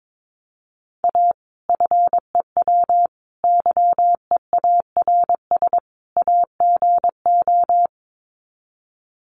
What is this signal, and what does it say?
Morse code, heard as a keyed tone, about 22 wpm, 700 Hz. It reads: A FEW YEARS AGO
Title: Sets of 4 Words 22wpm